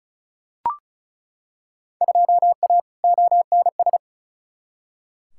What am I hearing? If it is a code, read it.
Morse code, heard as a keyed tone, about 34 wpm, 700 Hz. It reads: 2A ONS
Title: Field Day Exchanges Encore - 35wpm